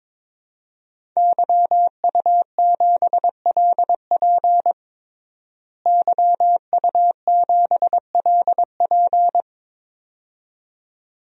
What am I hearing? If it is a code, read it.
Morse code, heard as a keyed tone, about 21 wpm, 700 Hz. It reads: YU7LP YU7LP
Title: DXCC-100 Call Signs - 5 Characters - Repeated 2x - 22wpm